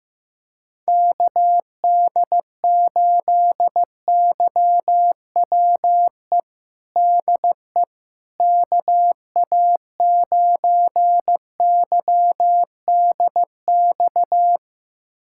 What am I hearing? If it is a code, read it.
Morse code, heard as a keyed tone, about 15 wpm, 700 Hz. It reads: KD8YWE DE KA9YDX
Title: USA Callsign DE Callsign 15wpm